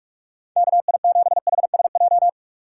Morse code, heard as a keyed tone, about 44 wpm, 700 Hz. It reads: K I 7 H S J
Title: USA Call Signs - 6 Characters - 6x Character Spacing - 45wpm